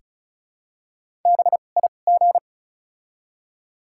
Morse code, heard as a keyed tone, about 34 wpm, spaced as for 25 wpm, 700 Hz. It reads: BIG
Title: 2x Character Spacing - Top 200 Words - 35wpm